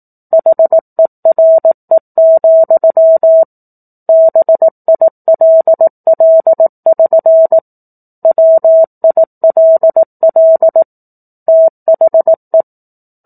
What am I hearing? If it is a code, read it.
Morse code, heard as a keyed tone, about 18 wpm, 650 Hz. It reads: HERE, BILL<SN> WILL THE